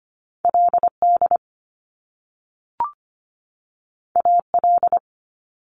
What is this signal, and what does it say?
Morse code, heard as a keyed tone, about 25 wpm, 700 Hz. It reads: LD AL
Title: Top Two Letter Combinations in Words 25wpm